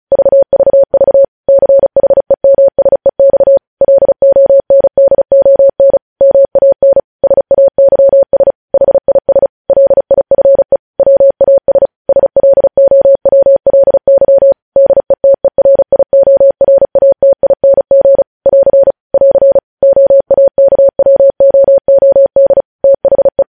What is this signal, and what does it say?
Morse code, heard as a keyed tone, about 36 wpm, 550 Hz. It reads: VVV CHEMSEX LONDON MAN SAYS HIS LIFE WAS SLOWLY DETERIORATING <AR> <AR> OAKWOOD THE
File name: podcast / News headlines-1x-35x35